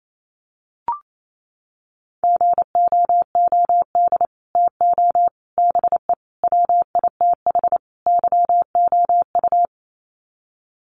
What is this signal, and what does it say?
Morse code, heard as a keyed tone, about 28 wpm, 700 Hz. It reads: GOOD TO BE WITH YOU
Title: Sets of 5 Words Encore2 28wpm